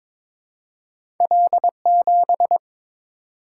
Morse code, heard as a keyed tone, about 22 wpm, 700 Hz. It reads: L7
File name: practice / 22wpm / Two Letter-Number Combinations - 22wpm